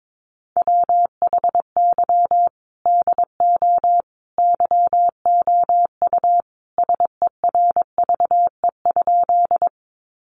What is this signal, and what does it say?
Morse code, heard as a keyed tone, about 22 wpm, 700 Hz. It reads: WHY DO YOU SERVE?
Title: Sentences from Top 400 Words 22wpm